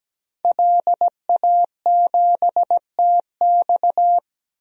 Morse code, heard as a keyed tone, about 17 wpm, 700 Hz. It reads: LA7TX
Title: DXCC-100 Call Signs - 5 Characters - 17wpm